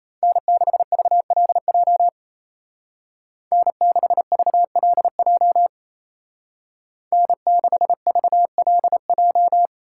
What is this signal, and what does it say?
Morse code, heard as a keyed tone, about 33 wpm, 700 Hz. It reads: N6VLJ N6VLJ N6VLJ
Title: Speed Racing - USA 1x3 Call Signs 28wpm